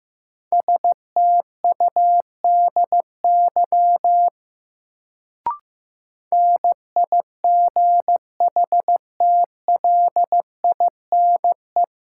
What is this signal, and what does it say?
Morse code, heard as a keyed tone, about 15 wpm, 700 Hz. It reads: STUDY NIGHTLINE
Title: Sets of 5 Words - New Word Drill - 15wpm